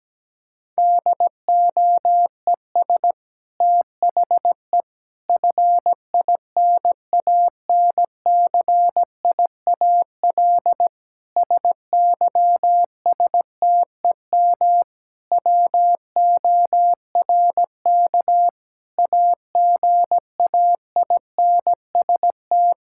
Morse code, heard as a keyed tone, about 17 wpm, 700 Hz. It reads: DOES THE FINANCIAL SYSTEM WORK AGAINST
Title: Sentences from Top 1000 Words - Review Part 5 - 17wpm